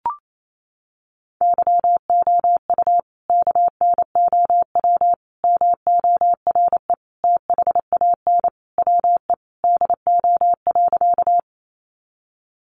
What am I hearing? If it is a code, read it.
Morse code, heard as a keyed tone, about 28 wpm, 700 Hz. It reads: YOU KNOW MORE THAN WE DO.
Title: Sentences from Top 100 Words 28wpm